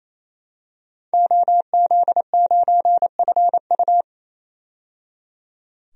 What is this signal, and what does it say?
Morse code, heard as a keyed tone, about 27 wpm, 700 Hz. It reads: OZ9FU
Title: DXCC-100 Call Signs - 5 Characters - 28wpm